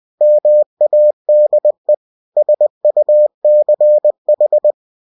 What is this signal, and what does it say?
Morse code, heard as a keyed tone, about 20 wpm, 600 Hz. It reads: MADE SUCH
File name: cw-books / winnie-the-pooh / winnie-20wpm-page-0053